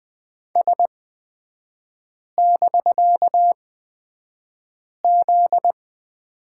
Morse code, heard as a keyed tone, about 20 wpm, spaced as for 5 wpm, 700 Hz. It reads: S<BK>Z
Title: 250 Lesson 40e - BK - Three Characters 20wpm